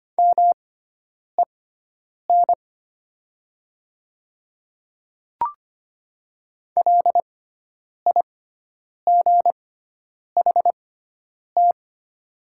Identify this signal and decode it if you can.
Morse code, heard as a keyed tone, about 25 wpm, spaced as for 9 wpm, 700 Hz. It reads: MEN LIGHT
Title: 6x Character Spacing - Top 300 Words - 25wpm